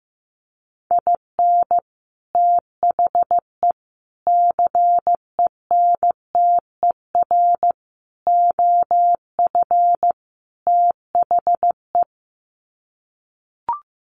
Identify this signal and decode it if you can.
Morse code, heard as a keyed tone, about 15 wpm, 700 Hz. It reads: IN THE CENTER OF THE
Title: Sets of 5 Words 15wpm